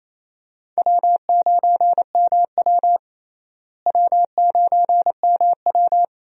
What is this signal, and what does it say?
Morse code, heard as a keyed tone, about 28 wpm, 700 Hz. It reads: W9MW W9MW
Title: USA Call Signs - 4 Characters - Repeated 2x - 28wpm